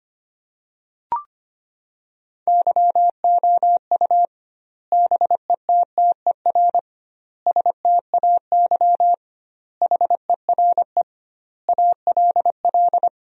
Morse code, heard as a keyed tone, about 25 wpm, 700 Hz. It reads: YOU BETTER STAY HERE ALL
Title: 2x Word-Spacing - Sentences from Top 500 Words - 25wpm